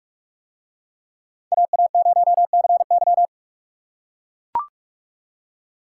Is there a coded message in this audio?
Yes: Morse code from 1.5 to 3.3 s, heard as a keyed tone, about 44 wpm, 700 Hz. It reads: AA0CY